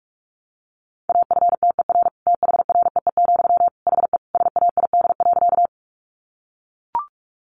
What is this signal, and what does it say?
Morse code, heard as a keyed tone, about 44 wpm, 700 Hz. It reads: AFTER THREE, HE SAID.
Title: Sentences from Top 200 Words 45wpm